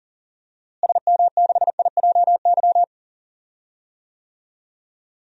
Morse code, heard as a keyed tone, about 40 wpm, 700 Hz. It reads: SM6IJY
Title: DXCC-100 Call Signs - 6 Characters - Repeated 2x - 40wpm